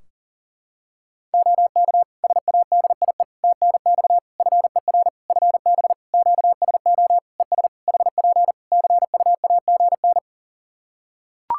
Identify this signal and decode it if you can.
Morse code, heard as a keyed tone, about 40 wpm, 700 Hz. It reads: OK SADIE TNX FER FB QSO ES HP CUAGN
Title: Practice QSO - Exchange 4 - Ending - 40wpm